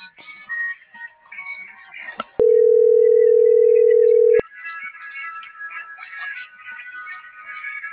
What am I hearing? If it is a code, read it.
Ringback tone (North America), 440 + 480 Hz.
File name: CENTRAL BLN / out ericka.canales - .61815